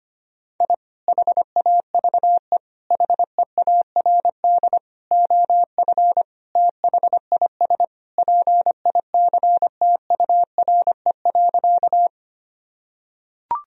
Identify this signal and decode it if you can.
Morse code, heard as a keyed tone, about 25 wpm, 700 Hz. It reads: I HAVE HEARD OF THIS PICTURE.
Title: Sentences from Top 1000 Words - Review Part 1 - 25wpm